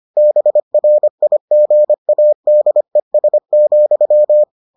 Morse code, heard as a keyed tone, about 25 wpm, 600 Hz. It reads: BRIGADES,